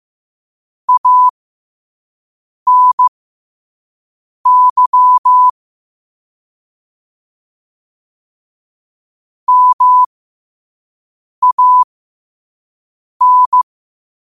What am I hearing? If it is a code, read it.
Morse code, heard as a keyed tone, about 15 wpm, spaced as for 5 wpm, 1000 Hz. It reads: ANY MAN
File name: qotdfiles / 5wpm